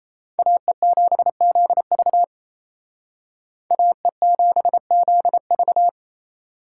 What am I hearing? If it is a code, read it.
Morse code, heard as a keyed tone, about 31 wpm, 700 Hz. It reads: AE7ZV AE7ZV
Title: Speed Racing - USA 2x2 Call Signs 28wpm